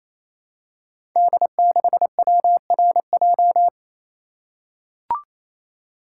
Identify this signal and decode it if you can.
Morse code, heard as a keyed tone, about 27 wpm, 700 Hz. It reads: D6WRJ
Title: Code Groups - 5 Letter:Numbers - 28wpm